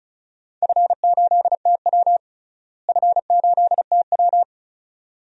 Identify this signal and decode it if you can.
Morse code, heard as a keyed tone, about 35 wpm, 700 Hz. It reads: F8TW F8TW
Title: DXCC-100 Call Signs - 4 Characters - Repeated 2x - 35wpm